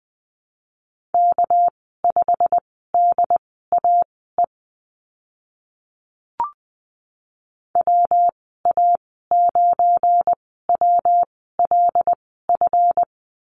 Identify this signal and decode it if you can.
Morse code, heard as a keyed tone, about 20 wpm, spaced as for 14 wpm, 700 Hz. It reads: K5DAE WA9WLF
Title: 219 Lesson 35g - Number 0 - Call Signs 20wpm